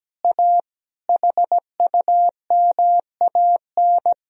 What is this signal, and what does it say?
Morse code, heard as a keyed tone, about 17 wpm, 700 Hz. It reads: A HUMAN
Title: Sentences from Top 1000 Words - Review Part 3 - 17wpm